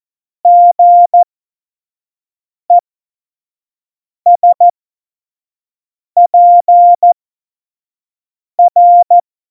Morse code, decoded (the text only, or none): GESPR